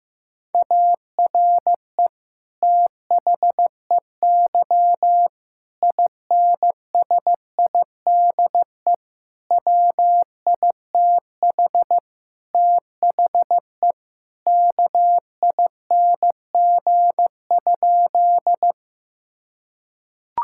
Morse code, heard as a keyed tone, about 15 wpm, 700 Hz. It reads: ARE THEY INSIDE WITH THE KING?
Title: Sentences from Top 1000 Words - Review Part 6 - 15wpm